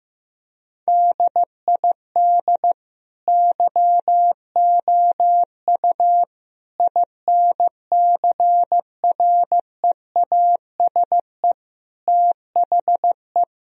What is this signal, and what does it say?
Morse code, heard as a keyed tone, about 16 wpm, 700 Hz. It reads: DID YOU INCREASE THE
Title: Sentences from Top 1000 Words - Review Part 4 - 15wpm